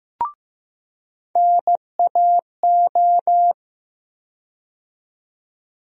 Morse code, heard as a keyed tone, about 15 wpm, 700 Hz. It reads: NAO
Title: USA Call Sign Postfixes - 3 Letters 15wpm